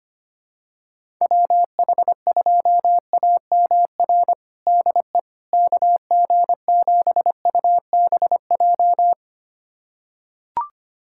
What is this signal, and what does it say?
Morse code, heard as a keyed tone, about 25 wpm, 700 Hz. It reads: WH2AMR DE KG7UBJ
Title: USA Callsign DE Callsign - 6 Characters - 25wpm